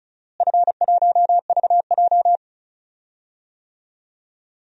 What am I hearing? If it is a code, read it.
Morse code, heard as a keyed tone, about 34 wpm, 700 Hz. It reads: F1VJ
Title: DXCC-100 Call Signs - 4 Characters - Repeated 3x - 35wpm